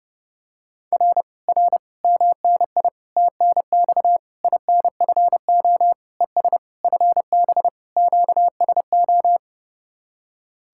Morse code, heard as a keyed tone, about 30 wpm, 700 Hz. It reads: R R MNI TNX INFO ES FB QSO